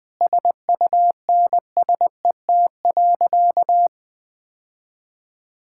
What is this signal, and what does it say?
Morse code, heard as a keyed tone, about 20 wpm, 700 Hz. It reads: SUNSET.
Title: Sentences with 7-Word Phrases - Spoken Beginning - 20wpm